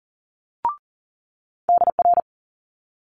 Morse code, heard as a keyed tone, about 39 wpm, 700 Hz. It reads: DR
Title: Multi-letter Phonemes Quick Fire 40wpm